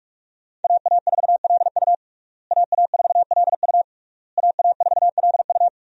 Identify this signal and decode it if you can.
Morse code, heard as a keyed tone, about 43 wpm, 700 Hz. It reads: AA4LU AA4LU AA4LU
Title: USA Call Signs - 5 Characters - Repeated 3x - 45wpm